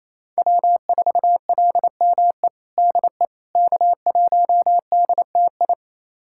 Morse code, heard as a keyed tone, about 28 wpm, 700 Hz. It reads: W4LME DE K1DTI
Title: USA Callsign DE Callsign - 5 Characters - 28wpm